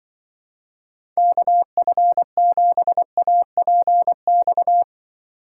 Morse code, heard as a keyed tone, about 23 wpm, 700 Hz. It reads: KF7APX
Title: Speed Racing - USA 2x3 Call Signs 20wpm